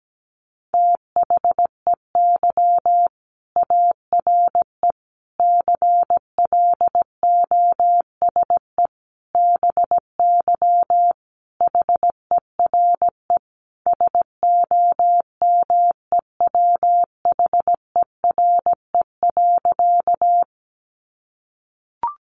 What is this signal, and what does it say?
Morse code, heard as a keyed tone, about 17 wpm, 700 Hz. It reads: THEY ARE CLOSE BY HERE SOMEWHERE.
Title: Sentences from Top 1400 Words 17wpm